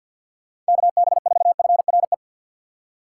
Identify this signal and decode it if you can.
Morse code, heard as a keyed tone, about 53 wpm, 700 Hz. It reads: KB4FRE
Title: USA Call Signs - 6 Characters - 6x Character Spacing - 50wpm